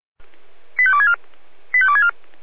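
Five-tone selective call (ZVEI), heard as quick runs of five tones, tones of about 70 ms. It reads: 85264 85264
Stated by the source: SDR receiver, ZVEI five-tone selective call, address 85310.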